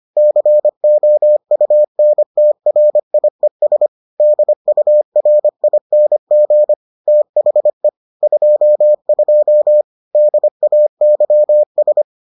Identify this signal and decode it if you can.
Morse code, heard as a keyed tone, about 25 wpm, 600 Hz. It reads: COUNTRIES DURING THE 22 DAYS